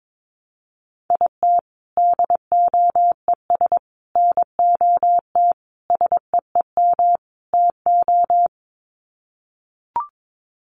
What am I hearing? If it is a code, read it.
Morse code, heard as a keyed tone, about 22 wpm, 700 Hz. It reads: IT DOES NOT SEEM TO
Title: Sets of 5 Words Encore2 22wpm